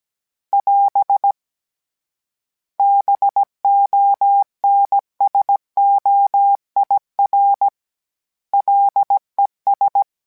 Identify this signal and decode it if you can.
Morse code, heard as a keyed tone, about 17 wpm, 800 Hz. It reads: <AS> BONSOIR LES